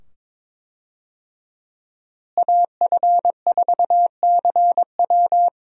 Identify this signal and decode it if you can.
Morse code, heard as a keyed tone, about 22 wpm, 700 Hz. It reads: AF4CW
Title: USA Call Signs - 5 Characters - Repeated 2x - 22wpm